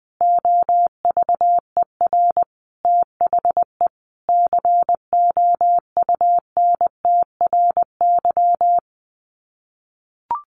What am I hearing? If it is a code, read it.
Morse code, heard as a keyed tone, about 20 wpm, 700 Hz. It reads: OVER THE COUNTRY
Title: Sets of 5 Words Encore 20wpm